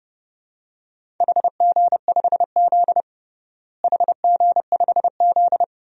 Morse code, heard as a keyed tone, about 32 wpm, 700 Hz. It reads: HG5Z HG5Z
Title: DXCC-100 Call Signs - 4 Characters - Repeated 2x - 30wpm